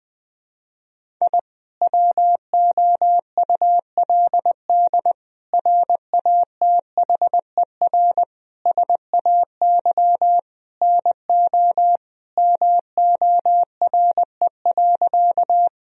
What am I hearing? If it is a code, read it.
Morse code, heard as a keyed tone, about 21 wpm, 700 Hz. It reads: I WOULD RATHER SAY NO MORE.
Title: Sentences from Top 1000 Words - Review Part 4 - 20wpm